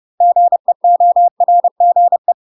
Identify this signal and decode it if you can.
Morse code, heard as a keyed tone, about 30 wpm, 700 Hz. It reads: GEORGE
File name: UK MASTER QSOs TRAINING / 30WPM / UK MASTER QSOs 0018